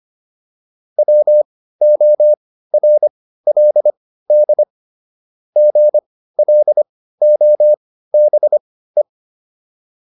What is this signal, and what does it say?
Morse code, heard as a keyed tone, about 25 wpm, spaced as for 15 wpm, 600 Hz. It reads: WORLD GLOBE